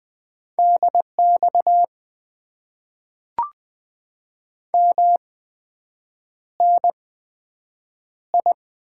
Morse code, heard as a keyed tone, about 20 wpm, spaced as for 7 wpm, 700 Hz. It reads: DX MNI
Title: Top 90 QSO Elements - 8x Character Spacing - 20wpm